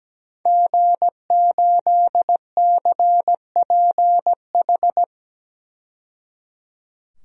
Morse code, heard as a keyed tone, about 17 wpm, 700 Hz. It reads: G8CPH
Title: DXCC-100 Call Signs - 5 Characters - 17wpm